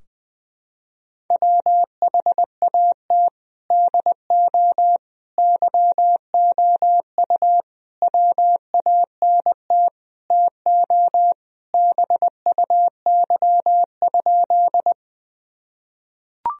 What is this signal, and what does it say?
Morse code, heard as a keyed tone, about 21 wpm, 700 Hz. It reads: WHAT DO YOU WANT TO BUY?
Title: Sentences from Top 1000 Words - Review Part 4 - 20wpm